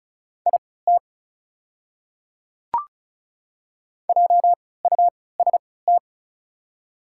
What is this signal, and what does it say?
Morse code, heard as a keyed tone, about 36 wpm, spaced as for 20 wpm, 700 Hz. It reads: IT JUST
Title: 3x Character Spacing - Top 500 Words - 35wpm